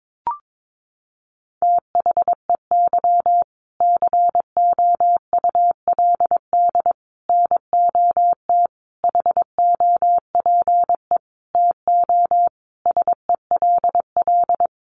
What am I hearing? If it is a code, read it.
Morse code, heard as a keyed tone, about 22 wpm, 700 Hz. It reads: THEY COULD NOT HOPE TO SELL
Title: Sentences from Top 1100 Words 22wpm